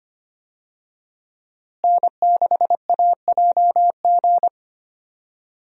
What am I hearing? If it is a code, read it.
Morse code, heard as a keyed tone, about 25 wpm, 700 Hz. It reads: N6AJG
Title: USA Call Signs - 5 Characters - Repeated 2x - 25wpm